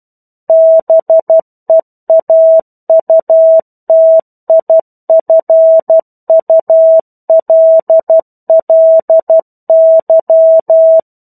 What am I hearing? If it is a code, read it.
Morse code, heard as a keyed tone, about 12 wpm, 650 Hz. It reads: BEAUTIFULLY
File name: 01-Chapter-01 12 WPM